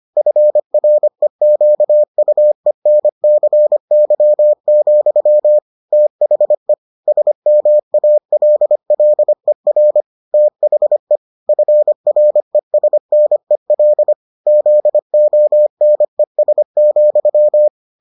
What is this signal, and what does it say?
Morse code, heard as a keyed tone, about 25 wpm, 600 Hz. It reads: FREQUENCY, THE SMALLER THE FRESNEL ZONES,